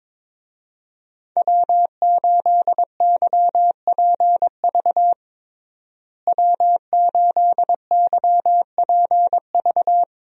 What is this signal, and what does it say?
Morse code, heard as a keyed tone, about 22 wpm, 700 Hz. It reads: W8YPV W8YPV
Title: USA Call Signs - 5 Characters - Repeated 2x - 22wpm